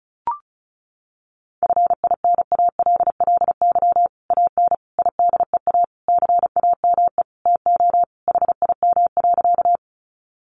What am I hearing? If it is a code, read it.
Morse code, heard as a keyed tone, about 34 wpm, 700 Hz. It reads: FINALLY AN IDEA CAME TO HIM.
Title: Sentences from Top 1000 Words 35wpm